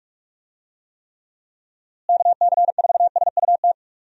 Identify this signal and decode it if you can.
Morse code, heard as a keyed tone, about 46 wpm, 700 Hz. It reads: KC4SUT
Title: USA Call Signs - 6 Characters - Repeated 2x - 45wpm